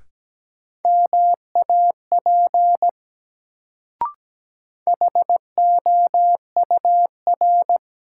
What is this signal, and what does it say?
Morse code, heard as a keyed tone, about 17 wpm, 700 Hz. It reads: MAP HOUR
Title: Sentences from Top 400 Words - New Word Drill - 17wpm